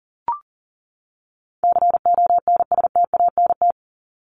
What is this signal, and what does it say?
Morse code, heard as a keyed tone, about 40 wpm, 700 Hz. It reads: CONSTANT
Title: Top 1251-1500 Words 40wpm